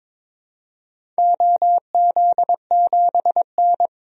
Morse code, heard as a keyed tone, about 22 wpm, 700 Hz. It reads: OZ7N
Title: DXCC-100 Call Signs - 4 Characters - 22wpm